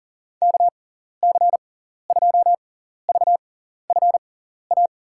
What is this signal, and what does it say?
Morse code, heard as a keyed tone, about 37 wpm, spaced as for 14 wpm, 700 Hz. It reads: KC2VFA
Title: USA Call Signs - 6 Characters - 6x Character Spacing - 40wpm